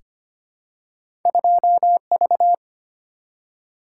Morse code, heard as a keyed tone, about 25 wpm, 700 Hz. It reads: 2V
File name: Two Letter-Number Combinations - 25wpm